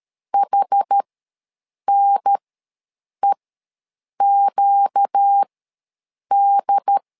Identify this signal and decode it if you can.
Morse code, heard as a keyed tone, about 13 wpm, spaced as for 7 wpm, 800 Hz. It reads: HNEQD
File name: LEZIONI / 4 Q1 ws 12wpm 2129